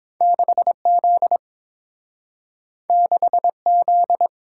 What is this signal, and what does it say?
Morse code, heard as a keyed tone, about 30 wpm, 700 Hz. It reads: 6Z 6Z